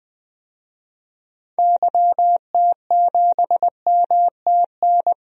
Morse code, heard as a keyed tone, about 20 wpm, 700 Hz. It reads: YT7MTN
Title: DXCC-100 Call Signs - 6 Characters - 20wpm